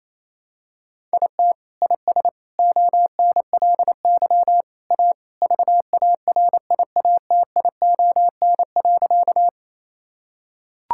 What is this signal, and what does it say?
Morse code, heard as a keyed tone, about 28 wpm, 700 Hz. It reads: IT IS ONLY A VARIATION.
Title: Sentences from Top 1600 Words 28wpm